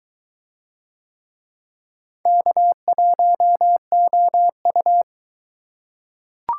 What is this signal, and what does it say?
Morse code, heard as a keyed tone, about 21 wpm, 700 Hz. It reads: K1OU